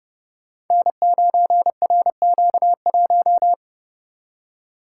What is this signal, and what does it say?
Morse code, heard as a keyed tone, about 30 wpm, 700 Hz. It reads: N9RQ1
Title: Code Groups - 5 Letter:Numbers - 30wpm